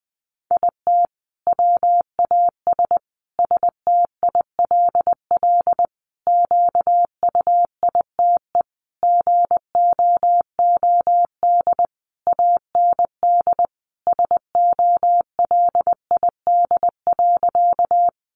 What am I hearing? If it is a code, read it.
Morse code, heard as a keyed tone, about 20 wpm, 700 Hz. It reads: IT WAS STILL QUITE GOOD AND SOLID.
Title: Sentences from Top 1500 Words 20wpm